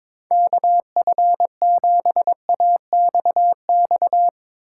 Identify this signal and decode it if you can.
Morse code, heard as a keyed tone, about 21 wpm, 700 Hz. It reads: KF7AXX
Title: USA Call Signs - 6 Characters - 4x Character Spacing - 22wpm